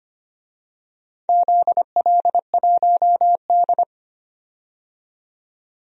Morse code, heard as a keyed tone, about 25 wpm, 700 Hz. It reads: ZL1D
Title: DXCC-100 Call Signs - 4 Characters - 25wpm